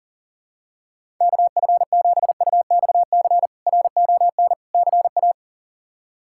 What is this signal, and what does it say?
Morse code, heard as a keyed tone, about 40 wpm, 700 Hz. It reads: KF7UXC RON CA
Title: K1USN SST QSO Practice - 40wpm